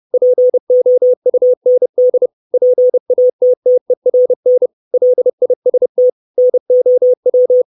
Morse code, heard as a keyed tone, about 30 wpm, 500 Hz. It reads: POUND PATTERN LIST NOW
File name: 632 Words 30WPM